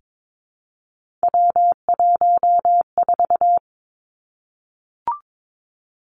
Morse code, heard as a keyed tone, about 22 wpm, 700 Hz. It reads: W14
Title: Three Challenging Letter-Number Combinations 22wpm